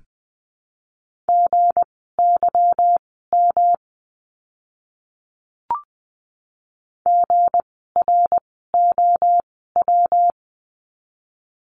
Morse code, heard as a keyed tone, about 19 wpm, spaced as for 14 wpm, 700 Hz. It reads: GYM GROW
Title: 160 Lesson 26f - Letter g - Words 20wpm